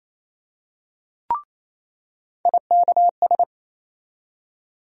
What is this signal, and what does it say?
Morse code, heard as a keyed tone, about 28 wpm, 700 Hz. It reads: IKS
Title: USA Call Sign Postfixes - 3 Letters 28wpm